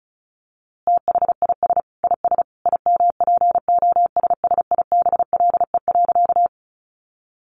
Morse code, heard as a keyed tone, about 34 wpm, 700 Hz. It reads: THIS IS IMPOSSIBLE.
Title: A Sound Of Thunder 35wpm